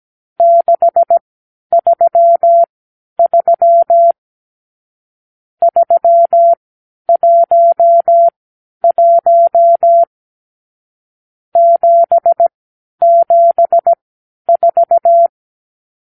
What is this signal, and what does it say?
Morse code, heard as a keyed tone, about 17 wpm, spaced as for 10 wpm, 680 Hz. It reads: 633 311 774
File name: k7qo / Lesson 52 (8)